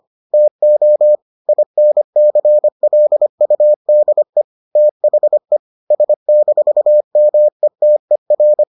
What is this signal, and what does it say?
Morse code, heard as a keyed tone, about 25 wpm, 600 Hz. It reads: TO INCLUDE THE S-METER